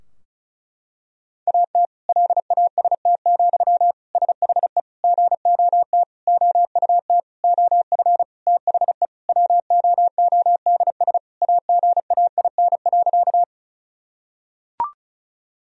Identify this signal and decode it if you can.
Morse code, heard as a keyed tone, about 36 wpm, 700 Hz. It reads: AT LAST, SHE GOT OUT OF THE WOODS AGAIN.
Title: Sentences from Top 1000 Words - Review Part 2 - 35wpm